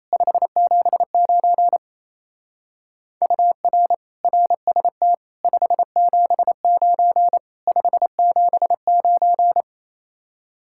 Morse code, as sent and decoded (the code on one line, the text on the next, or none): ..... --... ----. / ..- .-. / .-. ... - / ..... --... ----. / ..... --... ----.
579 UR RST 579 579